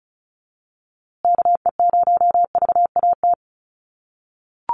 Morse code, heard as a keyed tone, about 34 wpm, 700 Hz. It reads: KE0VAT
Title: USA Call Signs Repeated 2x 35wpm